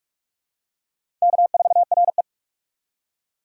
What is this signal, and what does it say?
Morse code, heard as a keyed tone, about 47 wpm, 700 Hz. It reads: K4RE
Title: USA Call Signs - 6x Character Spacing - 45wpm